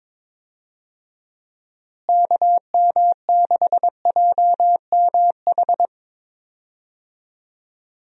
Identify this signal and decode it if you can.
Morse code, heard as a keyed tone, about 22 wpm, 700 Hz. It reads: KM6JMH
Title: USA Call Signs - 6 Characters - Repeated 3x - 22wpm